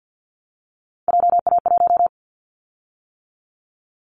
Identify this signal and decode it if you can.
Morse code, heard as a keyed tone, about 50 wpm, 700 Hz. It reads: JA1
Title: Callsign Prefixes for Top 50 DXCC - 50wpm